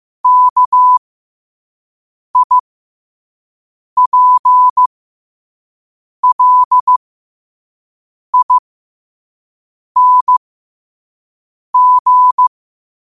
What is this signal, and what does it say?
Morse code, heard as a keyed tone, about 15 wpm, spaced as for 5 wpm, 1000 Hz. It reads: KIPLING
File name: qotdfiles / 5wpm